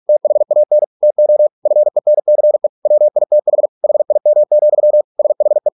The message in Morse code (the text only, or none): THAN TO FENCE WITH HIM, SHE